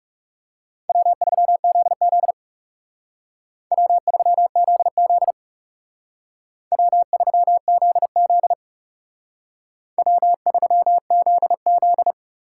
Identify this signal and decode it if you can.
Morse code, heard as a keyed tone, about 36 wpm, 700 Hz. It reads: W3ZZ W3ZZ W3ZZ W3ZZ
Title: Speed Racing - USA 1x2 Call Signs 30wpm